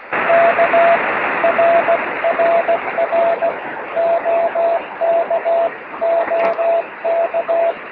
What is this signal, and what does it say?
Morse code, heard as a keyed tone, about 16 wpm, 660 Hz, with radio noise. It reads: K RRR OKOK
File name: sm7lcb / 27-es0u 23cm